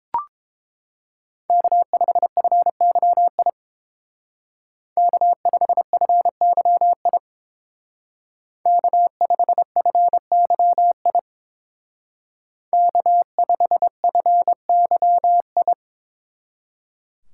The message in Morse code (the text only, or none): K5FYI K5FYI K5FYI K5FYI